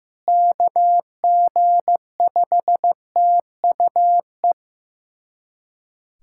Morse code, heard as a keyed tone, about 15 wpm, 700 Hz. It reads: KG5TUE
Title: USA Callsign DE Callsign 15wpm